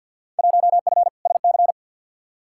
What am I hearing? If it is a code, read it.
Morse code, heard as a keyed tone, about 54 wpm, 700 Hz. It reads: 1F SC
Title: Field Day Exchanges Encore - 50wpm